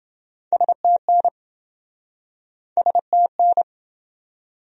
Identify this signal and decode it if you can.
Morse code, heard as a keyed tone, about 25 wpm, 700 Hz. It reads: STN STN
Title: Speed-Racing Abbreviations - 20wpm